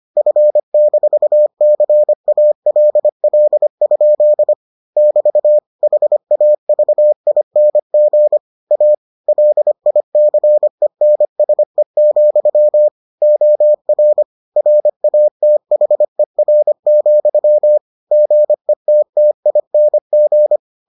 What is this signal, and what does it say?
Morse code, heard as a keyed tone, about 25 wpm, 600 Hz. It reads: F-CALL? <BT> HAVING A LICENSE, OR RATHER, GETTING